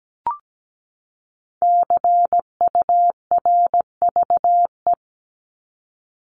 Morse code, heard as a keyed tone, about 17 wpm, 700 Hz. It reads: CURVE
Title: Top 1501-1750 Words 17wpm